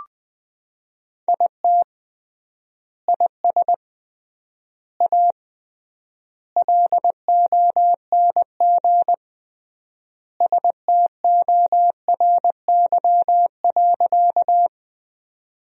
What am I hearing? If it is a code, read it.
Morse code, heard as a keyed tone, about 19 wpm, 700 Hz. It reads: IT IS A LONG STORY.
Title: 3x Word-Spacing - Sentences from Top 300 Words - 20wpm